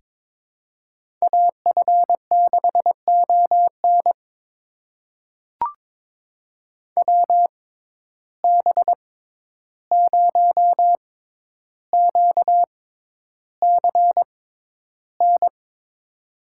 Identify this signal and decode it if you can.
Morse code, heard as a keyed tone, about 22 wpm, spaced as for 11 wpm, 700 Hz. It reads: AF6ON WB0QCN
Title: USA Call Signs - 6x Character Spacing - 22wpm